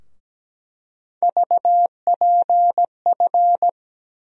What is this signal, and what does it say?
Morse code, heard as a keyed tone, about 17 wpm, 700 Hz. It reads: VPF
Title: USA Call Sign Postfixes - 3 Letters 17wpm